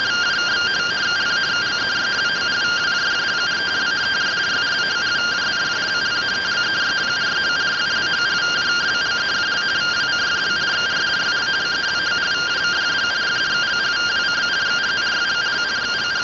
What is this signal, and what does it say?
Two-tone teleprinter signal, about 170 Hz shift.